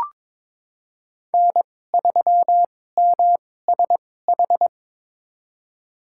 Morse code, heard as a keyed tone, about 22 wpm, spaced as for 16 wpm, 700 Hz. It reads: N3MSH
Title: USA Call Signs - 5 Characters - 2x Character Spacing - 22wpm